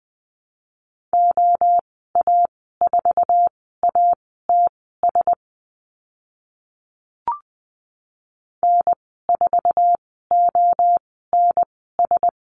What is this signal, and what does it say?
Morse code, heard as a keyed tone, about 20 wpm, spaced as for 14 wpm, 700 Hz. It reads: OA4ATS N4ONS
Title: 076 Lesson 13g - Letter L - Call Signs 20wpm